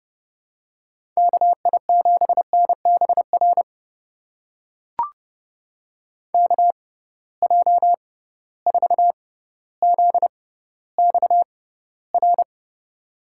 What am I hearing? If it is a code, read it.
Morse code, heard as a keyed tone, about 30 wpm, spaced as for 15 wpm, 700 Hz. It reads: KI7NBR KJ4ZXR
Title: USA Call Signs - 6x Character Spacing - 30wpm